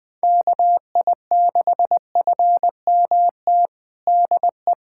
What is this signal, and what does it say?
Morse code, heard as a keyed tone, about 21 wpm, 700 Hz. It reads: KI6FMT DE
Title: USA Callsign DE Callsign - 6 Characters - 20wpm